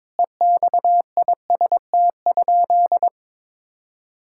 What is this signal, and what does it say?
Morse code, heard as a keyed tone, about 22 wpm, 700 Hz. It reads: EXIST?
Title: Sentences from Top 1000 Words - Review Part 5 - 22wpm